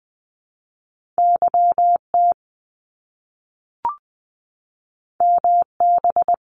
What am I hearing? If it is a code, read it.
Morse code, heard as a keyed tone, about 20 wpm, 700 Hz. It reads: YT MB
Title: Canadian Provinces and Territories - Repeated 3x - 20wpm